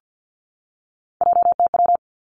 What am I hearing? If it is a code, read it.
Morse code, heard as a keyed tone, about 50 wpm, 700 Hz. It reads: JTW
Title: Three Challenging Letter-Number Combinations 50wpm